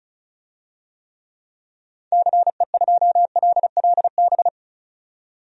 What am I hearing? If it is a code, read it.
Morse code, heard as a keyed tone, about 36 wpm, 700 Hz. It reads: CE2LLB
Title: DXCC-100 Call Signs - 6 Characters - 35wpm